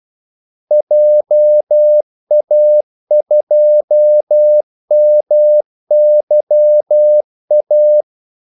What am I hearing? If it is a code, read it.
Morse code, heard as a keyed tone, about 12 wpm, 600 Hz. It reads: JA2MYA
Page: 2 ekstra ordmellemrum mellem kaldesignalerne